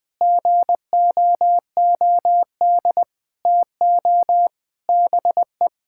Morse code, heard as a keyed tone, about 20 wpm, 700 Hz. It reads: GOOD TO BE
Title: Sets of 9 Words - 20wpm